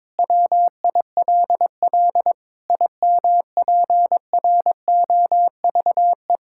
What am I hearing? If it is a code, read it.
Morse code, heard as a keyed tone, about 22 wpm, 700 Hz. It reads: WILL IMPROVE
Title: Sentences from Top 1000 Words - Review Part 5 - 22wpm